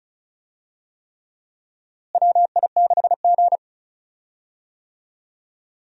Morse code, heard as a keyed tone, about 34 wpm, 700 Hz. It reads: WI6G
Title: USA Call Signs - 4 Characters - Repeated 3x - 35wpm